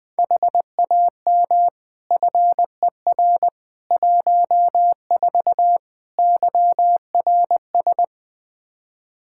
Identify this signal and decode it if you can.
Morse code, heard as a keyed tone, about 20 wpm, 700 Hz. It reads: HAM FER 14 YRS
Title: K1USN SST QSO Practice 20wpm